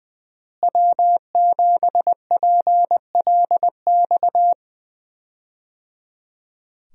Morse code, heard as a keyed tone, about 20 wpm, 700 Hz. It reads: W7PLX
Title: USA Call Signs - 5 Characters - Repeated 3x - 20wpm